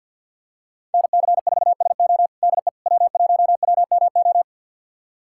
Morse code, heard as a keyed tone, about 51 wpm, 700 Hz. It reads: NC4SO DE W1WMO
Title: USA Callsign DE Callsign - 5 Characters - 50wpm